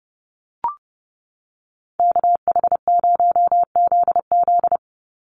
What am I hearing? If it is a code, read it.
Morse code, heard as a keyed tone, about 30 wpm, 700 Hz. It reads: KH0ZZ
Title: Call Signs - 30wpm